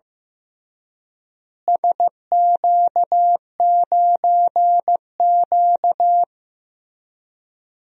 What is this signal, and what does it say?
Morse code, heard as a keyed tone, about 15 wpm, 700 Hz. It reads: SQ9Q
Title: DXCC-100 Call Signs - 4 Characters - Repeated 3x - 15wpm